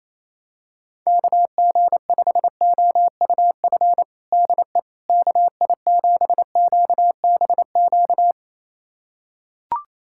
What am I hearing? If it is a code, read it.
Morse code, heard as a keyed tone, about 28 wpm, 700 Hz. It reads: KG5OUF DE KI7QBQ
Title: USA Callsign DE Callsign 28wpm